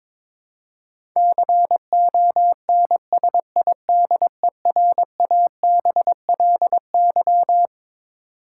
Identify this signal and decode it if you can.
Morse code, heard as a keyed tone, about 22 wpm, 700 Hz. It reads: CONSIDERABLY
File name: practice / 22wpm / Twelve Letter Words - 22wpm